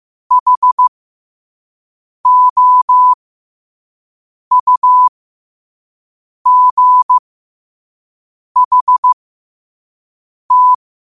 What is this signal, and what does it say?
Morse code, heard as a keyed tone, about 15 wpm, spaced as for 5 wpm, 1000 Hz. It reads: HOUGHT
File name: qotdfiles / 5wpm